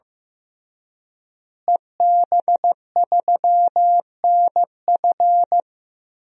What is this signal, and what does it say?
Morse code, heard as a keyed tone, about 16 wpm, 700 Hz. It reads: EB3NF
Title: DXCC-100 Call Signs - 5 Characters - Repeated 3x - 15wpm